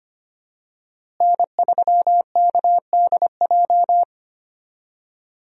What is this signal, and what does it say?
Morse code, heard as a keyed tone, about 25 wpm, 700 Hz. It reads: N3KDJ
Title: USA Call Signs - 5 Characters - Repeated 2x - 25wpm